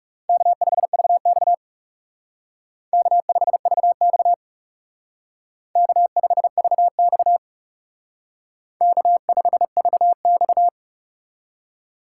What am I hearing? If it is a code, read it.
Morse code, heard as a keyed tone, about 36 wpm, 700 Hz. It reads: K5VX K5VX K5VX K5VX
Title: Speed Racing - USA 1x2 Call Signs 30wpm